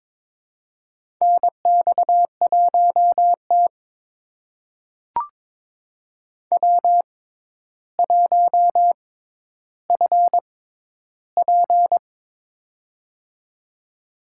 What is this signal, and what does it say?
Morse code, heard as a keyed tone, about 22 wpm, spaced as for 11 wpm, 700 Hz. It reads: NX1T W1FP
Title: USA Call Signs - 4 Characters - 6x Character Spacing - 22wpm